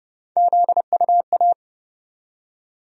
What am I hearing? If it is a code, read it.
Morse code, heard as a keyed tone, about 30 wpm, 700 Hz. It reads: ZUA
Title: USA Call Sign Postfixes - 3 Letters 30wpm